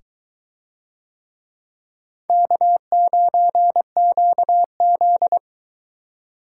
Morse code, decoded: K9QZ

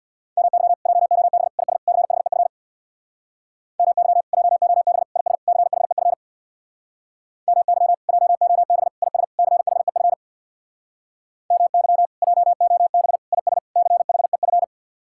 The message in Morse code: MY JOB IS CHEF MY JOB IS CHEF MY JOB IS CHEF MY JOB IS CHEF

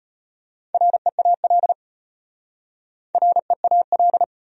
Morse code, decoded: REAL REAL

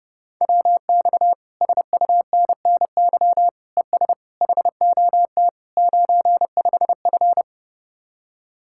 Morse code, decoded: WX SUNNY ES HOT 95F

